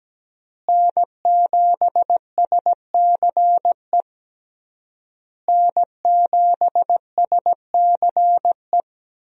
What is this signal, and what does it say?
Morse code, heard as a keyed tone, about 17 wpm, 700 Hz. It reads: N7SCE N7SCE
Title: USA Call Signs - 5 Characters - Repeated 3x - 17wpm